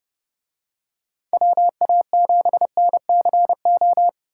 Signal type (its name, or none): Morse code